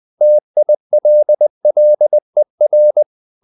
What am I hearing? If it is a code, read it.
Morse code, heard as a keyed tone, about 20 wpm, 600 Hz. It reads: TILLER
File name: the old man and the sea part 2 20wpm